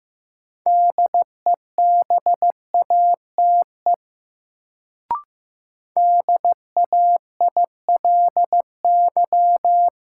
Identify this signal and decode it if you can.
Morse code, heard as a keyed tone, about 15 wpm, 700 Hz. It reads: DEBATE DAILY
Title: Sentences from Top 1100 Words - New Word Drill Alt - 15wpm